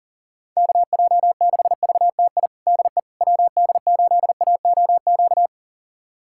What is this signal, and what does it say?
Morse code, heard as a keyed tone, about 40 wpm, 700 Hz. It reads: KJ6VTI DE WD8AOQ
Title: USA Callsign DE Callsign - 6 Characters - 40wpm